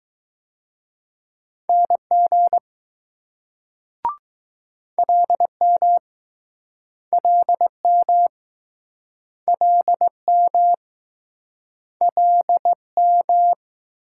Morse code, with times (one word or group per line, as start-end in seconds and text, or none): 1.7-2.6 NG
5.0-6.0 LM
7.1-8.3 LM
9.5-10.7 LM
12.0-13.5 LM